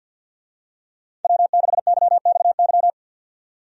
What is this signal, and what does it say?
Morse code, heard as a keyed tone, about 51 wpm, 700 Hz. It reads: W6YXY
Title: USA Call Signs - 5 Characters - 50wpm